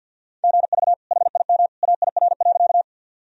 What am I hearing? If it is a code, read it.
Morse code, heard as a keyed tone, about 52 wpm, 700 Hz. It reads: GV HIM AIR.
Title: Sentences from Top 200 Words and Abbreviations 50wpm